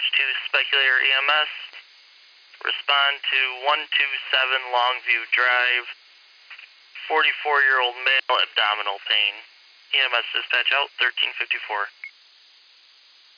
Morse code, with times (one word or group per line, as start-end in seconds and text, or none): none